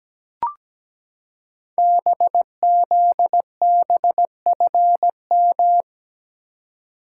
Morse code, heard as a keyed tone, about 17 wpm, 700 Hz. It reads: BZBFM
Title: Code Groups - 5 Letters - 17wpm